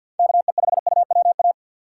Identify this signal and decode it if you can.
Morse code, heard as a keyed tone, about 51 wpm, 700 Hz. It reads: KE5UWA
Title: USA Call Signs - 6 Characters - 4x Character Spacing - 50wpm